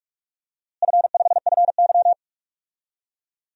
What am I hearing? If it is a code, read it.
Morse code, heard as a keyed tone, about 46 wpm, 700 Hz. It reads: F5FY
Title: DXCC-100 Call Signs - 4 Characters - Repeated 2x - 45wpm